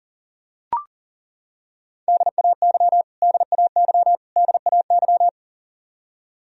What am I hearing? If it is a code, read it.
Morse code, heard as a keyed tone, about 40 wpm, 700 Hz. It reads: DAY DAY DAY
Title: Top 100 Words Repeated 40wpm